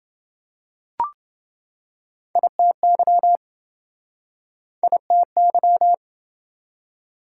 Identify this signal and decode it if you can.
Morse code, heard as a keyed tone, about 27 wpm, 700 Hz. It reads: ITY ITY